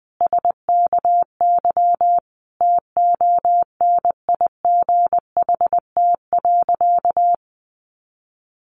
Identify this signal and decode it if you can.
Morse code, heard as a keyed tone, about 20 wpm, 700 Hz. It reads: SKY TONIGHT.
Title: Sentences from Top 1300 Words 20wpm